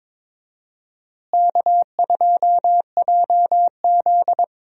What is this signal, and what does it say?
Morse code, heard as a keyed tone, about 22 wpm, 700 Hz. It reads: K2JZ
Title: USA Call Signs - 4 Characters - Repeated 3x - 22wpm